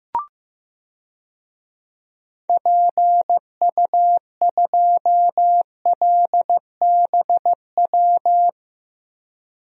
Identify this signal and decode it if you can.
Morse code, heard as a keyed tone, about 15 wpm, 700 Hz. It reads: PU2LBW
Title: DXCC-100 Call Signs - 6 Characters - Repeated 3x - 15wpm